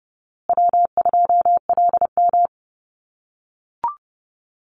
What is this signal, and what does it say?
Morse code, heard as a keyed tone, about 30 wpm, 700 Hz. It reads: W2LM
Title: USA Call Signs Repeated 3x 30wpm